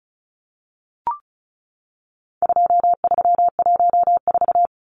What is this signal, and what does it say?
Morse code, heard as a keyed tone, about 34 wpm, 700 Hz. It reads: 2314
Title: Numbers - 4-Digit 35wpm